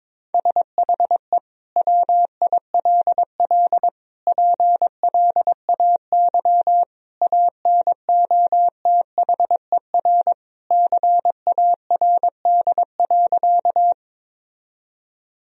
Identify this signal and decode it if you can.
Morse code, heard as a keyed tone, about 22 wpm, 700 Hz. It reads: SHE WILL PLAY ANOTHER CARD.
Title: Sentences from Top 1000 Words - Review Part 4 - 22wpm